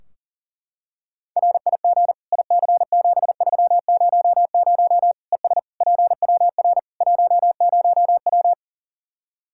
Morse code, heard as a keyed tone, about 40 wpm, 700 Hz. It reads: RIG IC7300 ES PWR 10W
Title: K1USN SST QSO Practice 40wpm